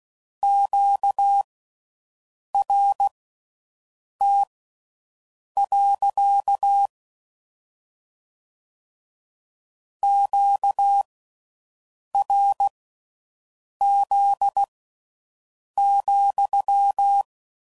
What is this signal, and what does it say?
Morse code, heard as a keyed tone, about 16 wpm, spaced as for 6 wpm, 800 Hz. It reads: QRT. QRZ,